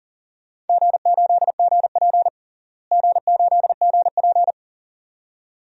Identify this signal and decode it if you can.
Morse code, heard as a keyed tone, about 40 wpm, 700 Hz. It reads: G8GP G8GP
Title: DXCC-100 Call Signs - 4 Characters - Repeated 2x - 40wpm